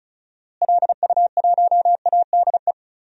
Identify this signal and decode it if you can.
Morse code, heard as a keyed tone, about 34 wpm, 700 Hz. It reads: LU1ADE
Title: DXCC-100 Call Signs - 6 Characters - Repeated 2x - 35wpm